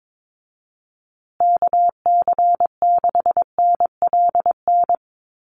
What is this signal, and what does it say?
Morse code, heard as a keyed tone, about 22 wpm, 700 Hz. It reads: KC6NLN
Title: USA Call Signs Repeated 3x 22wpm